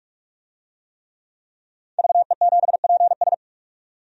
Morse code, heard as a keyed tone, about 46 wpm, 700 Hz. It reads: VE7PS
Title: DXCC-100 Call Signs - 5 Characters - 45wpm